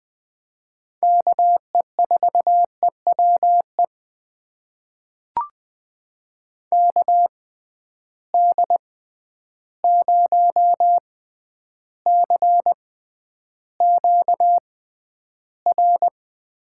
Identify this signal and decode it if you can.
Morse code, heard as a keyed tone, about 20 wpm, spaced as for 10 wpm, 700 Hz. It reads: KE4EWE KD0CQR
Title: USA Call Signs - 6 Characters - 6x Character Spacing - 20wpm